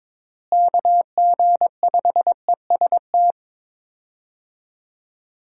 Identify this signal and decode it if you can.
Morse code, heard as a keyed tone, about 22 wpm, 700 Hz. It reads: KG5EST
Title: USA Call Signs - 6 Characters - Repeated 3x - 22wpm